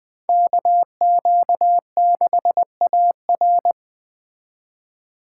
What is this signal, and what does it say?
Morse code, heard as a keyed tone, about 20 wpm, 700 Hz. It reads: KQ6AR
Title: Speed Racing - USA Call Signs 17wpm